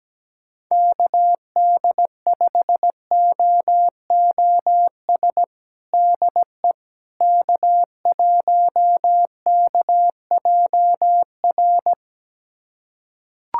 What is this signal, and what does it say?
Morse code, heard as a keyed tone, about 17 wpm, 700 Hz. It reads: KD5OOS DE K1KJR
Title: USA Callsign DE Callsign 17wpm